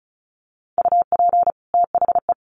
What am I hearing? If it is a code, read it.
Morse code, heard as a keyed tone, about 34 wpm, 700 Hz. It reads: UP THE
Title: Sets of 2 Words 35wpm